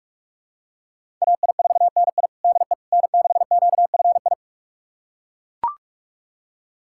Morse code, heard as a keyed tone, about 41 wpm, 700 Hz. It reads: AI4NI DE N6QFI
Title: USA Callsign DE Callsign - 5 Characters - 45wpm